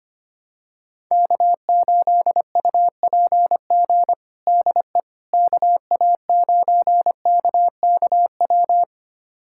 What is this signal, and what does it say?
Morse code, heard as a keyed tone, about 25 wpm, 700 Hz. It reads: K8UPG DE KA9KKW
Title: USA Callsign DE Callsign 25wpm